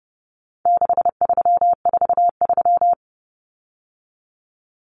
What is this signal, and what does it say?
Morse code, heard as a keyed tone, about 30 wpm, 700 Hz. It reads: 6343
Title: Numbers - 4-Digit 30wpm